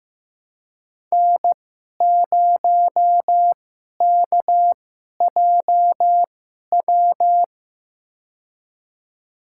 Morse code, heard as a keyed tone, about 15 wpm, spaced as for 11 wpm, 700 Hz. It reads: N0KJW
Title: USA Call Signs - 5 Characters - 2x Character Spacing - 15wpm